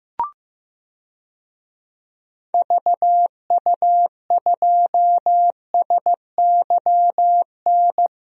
Morse code, heard as a keyed tone, about 15 wpm, 700 Hz. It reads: VU2SYN
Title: DXCC-100 Call Signs - 6 Characters - 15wpm